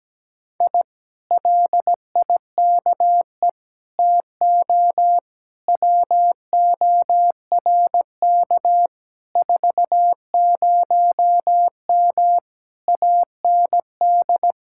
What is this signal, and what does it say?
Morse code, heard as a keyed tone, about 17 wpm, 700 Hz. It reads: I LIKE TO WORK 40M AND
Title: Speed-Racing Common QSO Phrases - 17wpm